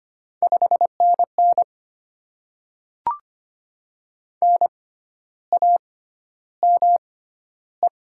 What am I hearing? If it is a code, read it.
Morse code, heard as a keyed tone, about 25 wpm, spaced as for 12 wpm, 700 Hz. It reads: 5NN NAME
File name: Top 20 QSO Elements - 6x Character Spacing - 25wpm